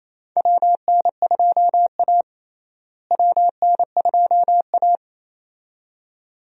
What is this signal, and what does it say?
Morse code, heard as a keyed tone, about 28 wpm, 700 Hz. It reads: WN2A WN2A
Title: USA Call Signs - 4 Characters - Repeated 2x - 28wpm